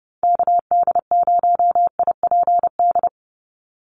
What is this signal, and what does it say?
Morse code, heard as a keyed tone, about 30 wpm, 700 Hz. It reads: KD0IPD
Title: USA Call Signs 30wpm